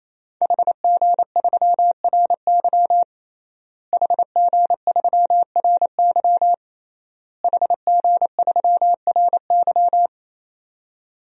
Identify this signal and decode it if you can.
Morse code, heard as a keyed tone, about 27 wpm, 700 Hz. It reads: HG3RY HG3RY HG3RY
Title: DXCC-100 Call Signs - 5 Characters - Repeated 3x - 28wpm